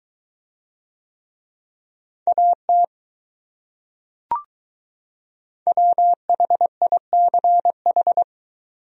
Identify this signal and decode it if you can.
Morse code, heard as a keyed tone, about 22 wpm, 700 Hz. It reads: AT WHICH